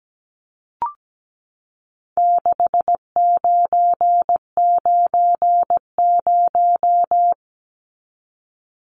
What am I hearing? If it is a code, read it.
Morse code, heard as a keyed tone, about 17 wpm, 700 Hz. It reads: 6990
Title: Numbers - 4-Digit 17wpm